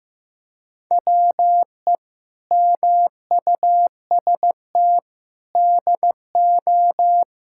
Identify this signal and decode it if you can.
Morse code, heard as a keyed tone, about 15 wpm, 700 Hz. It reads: WE MUST DO
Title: Sentences from Top 1000 Words - Review Part 5 - 15wpm